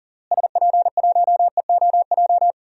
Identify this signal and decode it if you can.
Morse code, heard as a keyed tone, about 40 wpm, 700 Hz. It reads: SP1EOJ
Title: DXCC-100 Call Signs - 6 Characters - Repeated 3x - 40wpm